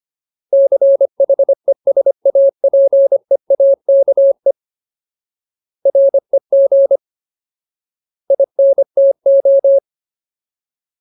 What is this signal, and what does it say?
Morse code, heard as a keyed tone, about 25 wpm, 550 Hz. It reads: CHESAPEAKE REG INTO
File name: Ultimate Word List / 25WPM Ultimate